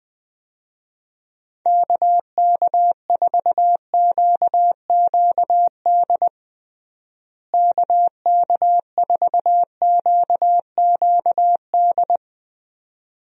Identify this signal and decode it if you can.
Morse code, heard as a keyed tone, about 21 wpm, 700 Hz. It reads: KK4QQD KK4QQD
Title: USA Call Signs - 6 Characters - Repeated 2x - 20wpm